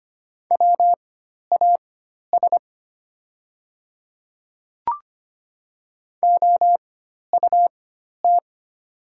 Morse code, heard as a keyed tone, about 25 wpm, spaced as for 12 wpm, 700 Hz. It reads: WAS OUT
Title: 4x Character Spacing - Top 100 Words - 25wpm